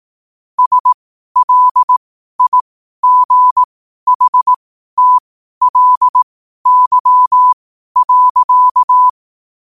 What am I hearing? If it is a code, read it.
Morse code, heard as a keyed tone, about 18 wpm, spaced as for 12 wpm, 1000 Hz. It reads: SLIGHTLY.
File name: qotdfiles / 12wpm